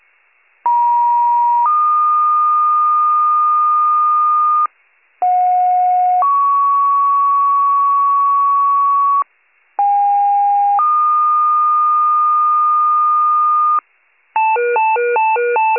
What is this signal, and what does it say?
Two-tone page, heard as a one-second tone, then a three-second tone, about 950 then 1220 Hz, about 730 then 1120 Hz, about 810 then 1220 Hz. It is heard from 0.6 to 13.8 s.